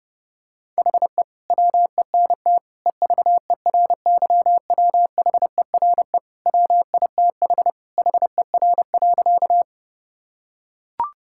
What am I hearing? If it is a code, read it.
Morse code, heard as a keyed tone, about 30 wpm, 700 Hz. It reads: HE WENT EVERYWHERE WITH HER.
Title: Sentences from Top 1600 Words 30wpm